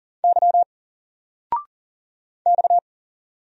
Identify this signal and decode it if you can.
Morse code, heard as a keyed tone, about 40 wpm, 700 Hz. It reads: Y X
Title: Single Letter-Number 40wpm